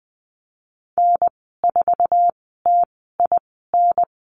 Morse code, heard as a keyed tone, about 20 wpm, 700 Hz. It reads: N 4 T I N
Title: 055 Lesson 10g - Letter r - Call Signs 20wpm